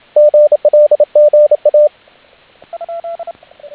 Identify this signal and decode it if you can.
Morse code, heard as a keyed tone, about 28 wpm, 590 Hz, with radio noise. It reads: GLGA